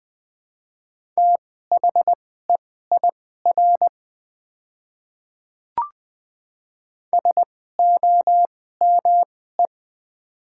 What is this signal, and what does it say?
Morse code, heard as a keyed tone, about 20 wpm, spaced as for 15 wpm, 700 Hz. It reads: THEIR SOME
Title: 2x Character Spacing - Top 200 Words - 20wpm